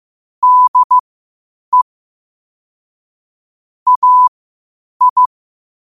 Morse code, heard as a keyed tone, about 15 wpm, spaced as for 8 wpm, 1000 Hz. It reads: DE AI
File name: qotdfiles / 8wpm